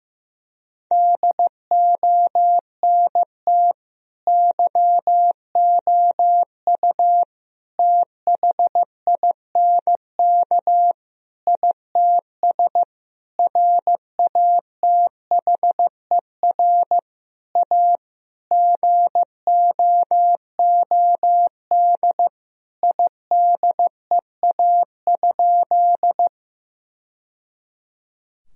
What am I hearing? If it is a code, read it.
Morse code, heard as a keyed tone, about 16 wpm, 700 Hz. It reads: DONT YOU THINK ITS RATHER A GOOD IDEA?
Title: Sentences from Top 1000 Words - Review Part 4 - 15wpm